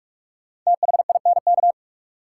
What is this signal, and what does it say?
Morse code, heard as a keyed tone, about 47 wpm, 700 Hz. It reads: THINK
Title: Sentences from Top 200 Words - New Word Drill - 45wpm